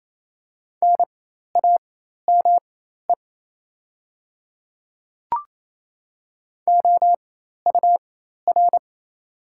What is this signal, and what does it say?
Morse code, heard as a keyed tone, about 28 wpm, spaced as for 13 wpm, 700 Hz. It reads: NAME OUR
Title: 4x Character Spacing - Top 500 Words - 28wpm